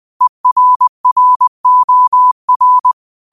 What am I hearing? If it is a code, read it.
Morse code, heard as a keyed tone, about 20 wpm, 1000 Hz. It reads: ERROR
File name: qotdfiles / 20wpm